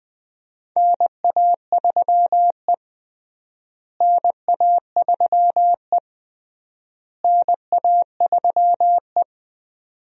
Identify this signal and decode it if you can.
Morse code, heard as a keyed tone, about 21 wpm, 700 Hz. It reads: NA3E NA3E NA3E
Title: USA Call Signs - 4 Characters - Repeated 3x - 20wpm